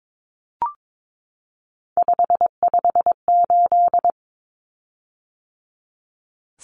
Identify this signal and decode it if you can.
Morse code, heard as a keyed tone, about 22 wpm, 700 Hz. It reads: 558
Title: Common Cut Numbers - 3-Digit 22wpm